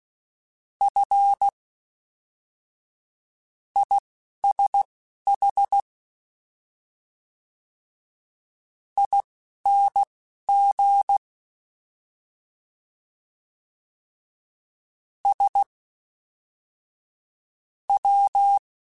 Morse code, heard as a keyed tone, about 16 wpm, spaced as for 11 wpm, 800 Hz. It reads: F ISH ING S W
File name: Misc / Misc-mp3s / Jumps @10